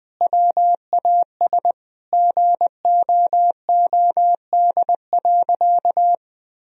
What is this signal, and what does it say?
Morse code, heard as a keyed tone, about 20 wpm, 700 Hz. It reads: WAS GOOD.
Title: Sentences from Top 1000 Words - Review Part 3 - 20wpm